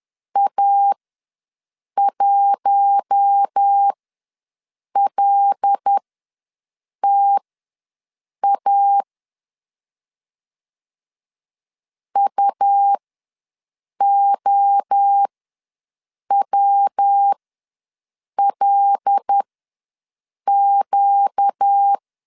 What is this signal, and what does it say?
Morse code, heard as a keyed tone, about 10 wpm, spaced as for 6 wpm, 800 Hz. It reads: A1LTA UOWLQ
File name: LEZIONI / 5 LBVC / 5 L ws 10wpm 3636